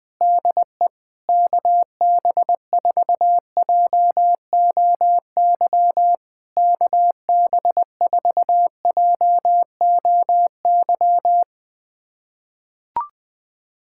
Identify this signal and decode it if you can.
Morse code, heard as a keyed tone, about 20 wpm, 700 Hz. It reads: DE KB4JOY KB4JOY
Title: POTA QSO Practice - 20wpm